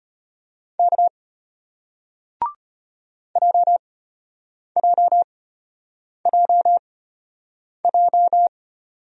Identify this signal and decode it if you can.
Morse code, heard as a keyed tone, about 33 wpm, spaced as for 8 wpm, 700 Hz. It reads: K JJJJ